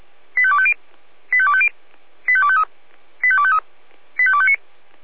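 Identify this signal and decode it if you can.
Five-tone selective call (ZVEI), heard as quick runs of five tones, tones of about 70 ms. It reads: 85250 85250 85253 85253 85259
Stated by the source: SDR recording of a ZVEI five-tone selective call, address 85252.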